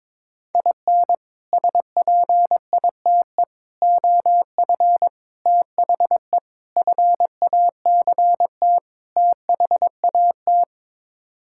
Morse code, heard as a keyed tone, about 21 wpm, 700 Hz. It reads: IN SPITE OF THE FACT THAT
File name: practice / 22wpm / Sets of 6 Words - 22wpm